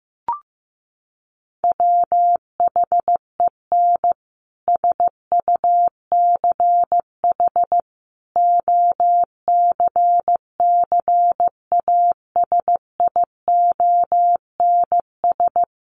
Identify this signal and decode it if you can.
Morse code, heard as a keyed tone, about 15 wpm, 700 Hz. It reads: WHEN SUCH OCCASIONS
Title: A Sound Of Thunder 15wpm